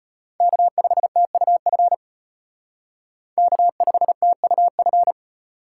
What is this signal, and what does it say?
Morse code, heard as a keyed tone, about 32 wpm, 700 Hz. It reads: K5TUF K5TUF